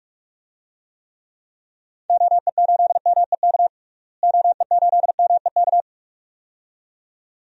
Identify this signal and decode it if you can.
Morse code, heard as a keyed tone, about 46 wpm, 700 Hz. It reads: OE8MEK OE8MEK
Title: DXCC-100 Call Signs - 6 Characters - Repeated 2x - 45wpm